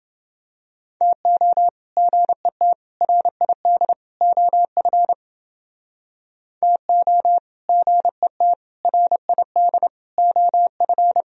Morse code, heard as a keyed tone, about 29 wpm, 700 Hz. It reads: TO GET RID OF TO GET RID OF